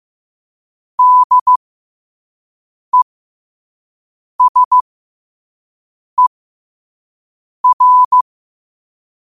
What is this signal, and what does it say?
Morse code, heard as a keyed tone, about 15 wpm, spaced as for 5 wpm, 1000 Hz. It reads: DESER
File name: qotdfiles / 5wpm